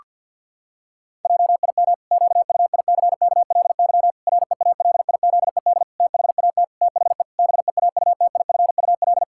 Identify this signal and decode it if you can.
Morse code, heard as a keyed tone, about 50 wpm, 700 Hz. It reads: JIM QUICKLY REALIZED THAT THE BEAUTIFUL
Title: Pangram Challenge v2 - 50wpm